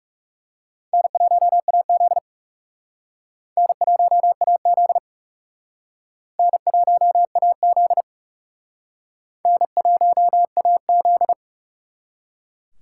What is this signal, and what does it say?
Morse code, heard as a keyed tone, about 38 wpm, 700 Hz. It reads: N1AZ N1AZ N1AZ N1AZ